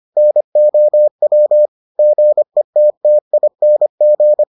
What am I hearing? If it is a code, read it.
Morse code, heard as a keyed tone, about 25 wpm, 600 Hz. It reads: NOW GETTING